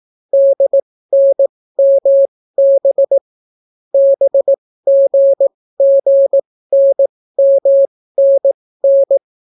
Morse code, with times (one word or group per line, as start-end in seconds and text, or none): 0.3-3.2 DNMB
3.9-9.2 BGGNMNN